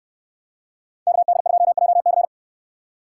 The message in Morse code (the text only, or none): KD2JC